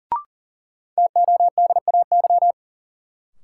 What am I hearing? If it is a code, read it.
Morse code, heard as a keyed tone, about 37 wpm, 700 Hz. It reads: TODAY
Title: Sets of 5 Words Encore - New Word Drill - 40wpm